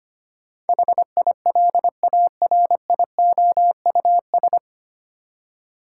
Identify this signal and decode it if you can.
Morse code, heard as a keyed tone, about 25 wpm, 700 Hz. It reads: HILARIOUS
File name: Nine Letter Words - 25wpm